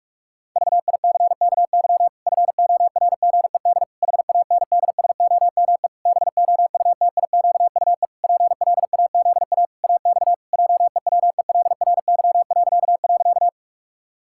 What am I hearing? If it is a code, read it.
Morse code, heard as a keyed tone, about 46 wpm, 700 Hz. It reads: VICKY FORGED HANDSOME BOUTIQUE PLAZA AX JEWELRY..
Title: Pangram Challenge v2 - 45wpm